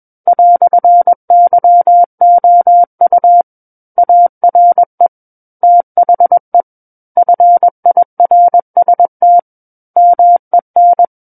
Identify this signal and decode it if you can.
Morse code, heard as a keyed tone, about 21 wpm, 700 Hz. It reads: "YOU ARE THE FIRST MEN
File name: Book One / 12-Chapter 12 - 21 WPM